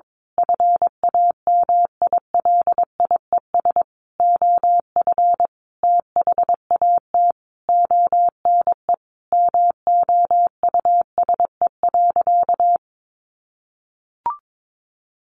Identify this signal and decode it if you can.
Morse code, heard as a keyed tone, about 22 wpm, 700 Hz. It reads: FAMILIES OF THAT ONE MOUSE.
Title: A Sound Of Thunder 22wpm